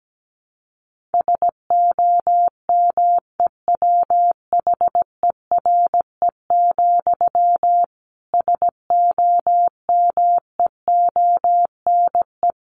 Morse code, heard as a keyed tone, about 17 wpm, 700 Hz. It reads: SOMEWHERE, SOMEONE
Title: A Sound Of Thunder 17wpm